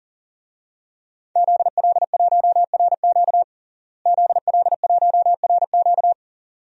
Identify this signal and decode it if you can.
Morse code, heard as a keyed tone, about 40 wpm, 700 Hz. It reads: ZL1RQ ZL1RQ
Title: DXCC-100 Call Signs - 5 Characters - Repeated 2x - 40wpm